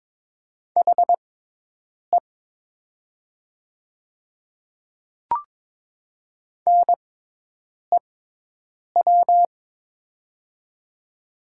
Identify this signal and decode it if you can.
Morse code, heard as a keyed tone, about 22 wpm, spaced as for 8 wpm, 700 Hz. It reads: HE NEW
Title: 6x Character Spacing - Top 300 Words - 22wpm